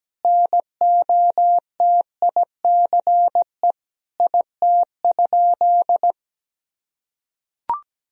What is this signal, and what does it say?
Morse code, heard as a keyed tone, about 17 wpm, 700 Hz. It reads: NOTICE IT?
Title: Sentences from Top 1000 Words - Review Part 2 - 17wpm